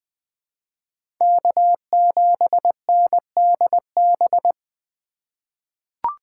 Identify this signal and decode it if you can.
Morse code, heard as a keyed tone, about 21 wpm, 700 Hz. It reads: K7NDB
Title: USA Call Signs - 5 Characters - 8x Character Spacing - 20wpm